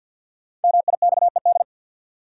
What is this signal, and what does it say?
Morse code, heard as a keyed tone, about 52 wpm, 700 Hz. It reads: MIXED
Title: Sentences from Top 1500 Words - New Word Drill - 50wpm